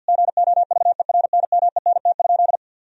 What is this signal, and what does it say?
Morse code, heard as a keyed tone, about 50 wpm, 700 Hz. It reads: GOVERNMENT?
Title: Sentences from Top 1000 Words - Review Part 4 - 50wpm